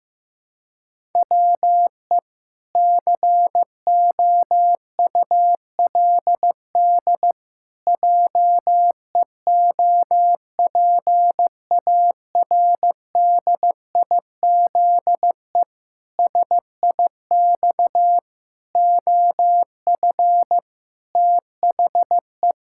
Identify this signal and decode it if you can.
Morse code, heard as a keyed tone, about 16 wpm, 700 Hz. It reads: WE COULD JEOPARDIZE SIX OF THE
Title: Pangram Challenge v2 - 15wpm